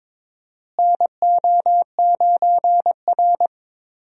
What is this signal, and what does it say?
Morse code, heard as a keyed tone, about 22 wpm, 700 Hz. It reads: NO9R
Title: USA Call Signs - 4 Characters - Repeated 2x - 22wpm